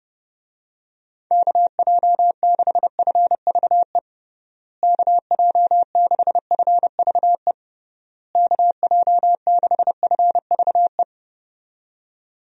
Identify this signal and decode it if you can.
Morse code, heard as a keyed tone, about 32 wpm, 700 Hz. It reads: KJ6FVE KJ6FVE KJ6FVE
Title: USA Call Signs - 6 Characters - Repeated 3x - 30wpm